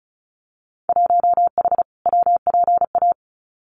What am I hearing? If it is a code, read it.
Morse code, heard as a keyed tone, about 34 wpm, 700 Hz. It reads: 1H WPA
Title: Winter Field Day Exchanges 35wpm